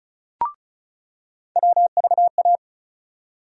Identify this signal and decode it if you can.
Morse code, heard as a keyed tone, about 36 wpm, 700 Hz. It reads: WVA
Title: USA Call Sign Postfixes - 3 Letters 35wpm